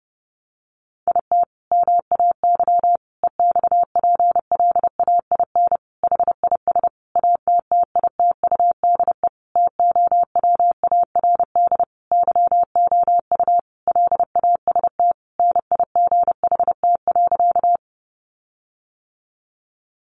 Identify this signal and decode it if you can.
Morse code, heard as a keyed tone, about 30 wpm, 700 Hz. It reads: IT MAY EXPLAIN HIS ATTITUDE TOWARD YOU LAST NIGHT.
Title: Sentences from Top 1300 Words 30wpm